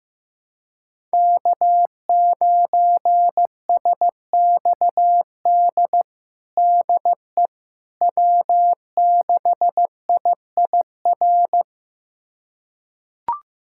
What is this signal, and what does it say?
Morse code, heard as a keyed tone, about 15 wpm, 700 Hz. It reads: K9SXD DE W6IIR
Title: USA Callsign DE Callsign - 5 Characters - 15wpm